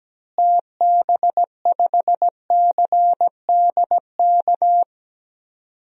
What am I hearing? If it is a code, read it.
Morse code, heard as a keyed tone, about 17 wpm, 700 Hz. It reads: TB5CDK
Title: DXCC-100 Call Signs - 6 Characters - Repeated 2x - 17wpm